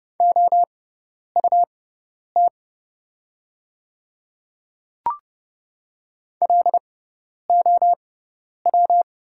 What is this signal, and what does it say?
Morse code, heard as a keyed tone, about 30 wpm, spaced as for 10 wpm, 690 Hz. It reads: OUT LOW
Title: 6x Character Spacing - Top 200 Words - 30wpm